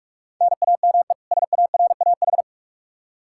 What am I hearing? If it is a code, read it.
Morse code, heard as a keyed tone, about 46 wpm, 700 Hz. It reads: NAME SARAH
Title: K1USN SST QSO Practice 45wpm